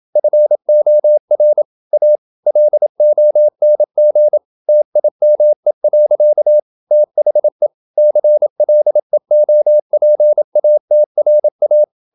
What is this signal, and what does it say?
Morse code, heard as a keyed tone, about 28 wpm, 600 Hz. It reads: FOR A LONG TIME. THE CLEOPATRA
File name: Never 0027wpm